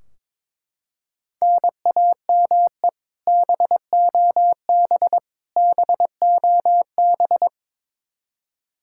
Morse code, heard as a keyed tone, about 22 wpm, 700 Hz. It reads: NAME BOB BOB
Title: Practice QSO - Exchange 1 - Beginning - 22wpm